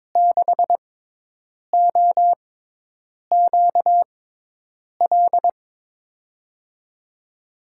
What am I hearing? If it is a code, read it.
Morse code, heard as a keyed tone, about 22 wpm, spaced as for 8 wpm, 700 Hz. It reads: 6OQL
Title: USA Call Signs - 6 Characters - 6x Character Spacing - 22wpm